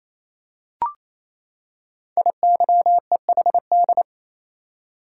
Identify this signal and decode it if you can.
Morse code, heard as a keyed tone, about 27 wpm, 700 Hz. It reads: IYEHD
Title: Code Groups - 5 Letters - 28wpm